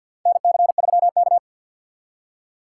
Morse code, heard as a keyed tone, about 52 wpm, 700 Hz. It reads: NC3K
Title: USA Call Signs - 4 Characters - 6x Character Spacing - 50wpm